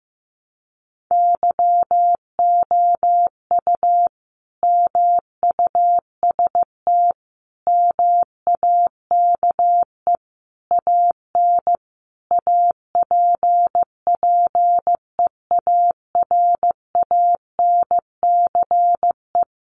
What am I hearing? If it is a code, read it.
Morse code, heard as a keyed tone, about 14 wpm, 700 Hz. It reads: YOU MUST MAKE AN APPEARANCE
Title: Sentences from Top 1100 Words 15wpm